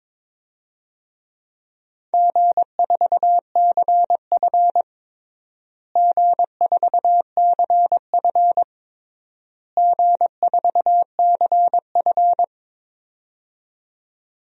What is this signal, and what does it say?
Morse code, heard as a keyed tone, about 21 wpm, 700 Hz. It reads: G4CF G4CF G4CF
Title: DXCC-100 Call Signs - 4 Characters - Repeated 3x - 22wpm